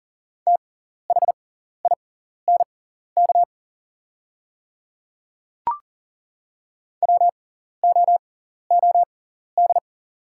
Morse code, heard as a keyed tone, about 40 wpm, spaced as for 14 wpm, 700 Hz. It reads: THINK WOOD
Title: 6x Character Spacing - Top 300 Words - 40wpm